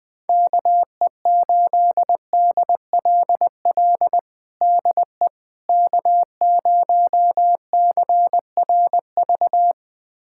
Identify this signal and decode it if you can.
Morse code, heard as a keyed tone, about 21 wpm, 700 Hz. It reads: KE8DLL DE K0CRV
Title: USA Callsign DE Callsign 20wpm